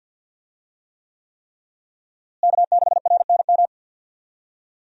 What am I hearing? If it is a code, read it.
Morse code, heard as a keyed tone, about 51 wpm, 700 Hz. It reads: K6RNM
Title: USA Call Signs - 5 Characters - Repeated 3x - 50wpm